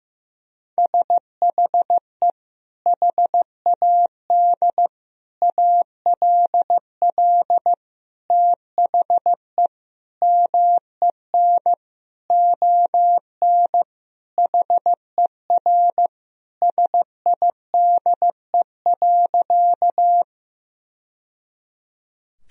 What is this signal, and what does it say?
Morse code, heard as a keyed tone, about 15 wpm, 700 Hz. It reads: SHE HAD ALL THE MEN ON HER SIDE.
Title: Sentences from Top 1000 Words - Review Part 1 - 15wpm